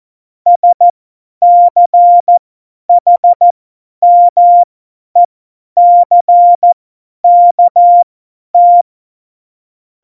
Morse code, heard as a keyed tone, about 14 wpm, spaced as for 10 wpm, 700 Hz. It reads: SCHMECKT
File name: Morsekurs / Witze / Witz 03.Dezember 2019, 50 F70